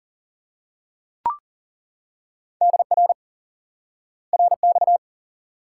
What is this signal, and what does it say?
Morse code, heard as a keyed tone, about 40 wpm, 700 Hz. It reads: DR RX
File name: IQE Course 3 - Lesson 89 - YAGI - Two Elements - 40wpm